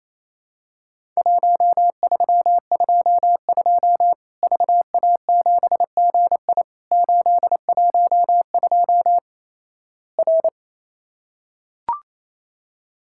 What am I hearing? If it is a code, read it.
Morse code, heard as a keyed tone, about 28 wpm, 700 Hz. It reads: 1322 VA7GI 812 R
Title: Worked All Europe DX Contest - QTC Exchange - 28wpm